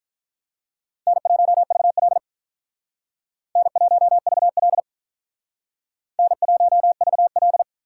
Morse code, heard as a keyed tone, about 44 wpm, 700 Hz. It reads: N1VL N1VL N1VL